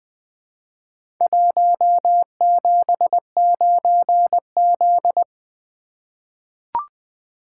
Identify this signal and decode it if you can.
Morse code, heard as a keyed tone, about 20 wpm, 700 Hz. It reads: 179Z
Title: Four Letter-Number Combinations - 20wpm